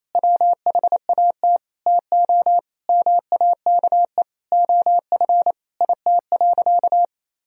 Morse code, heard as a keyed tone, about 28 wpm, 700 Hz. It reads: WHAT TO MAKE OF IT.
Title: Sentences from Top 1000 Words - Review Part 1 - 28wpm